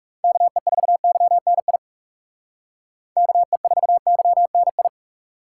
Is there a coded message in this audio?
Yes: Morse code from 0.2 to 4.9 s, heard as a keyed tone, about 38 wpm, 700 Hz. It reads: KE4YNI KE4YNI